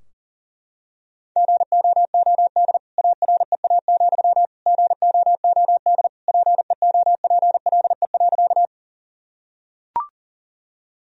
Morse code, heard as a keyed tone, about 37 wpm, 700 Hz. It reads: GOOD AREA, GOOD PEOPLE.
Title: Sentences from Top 1000 Words - Review Part 2 - 40wpm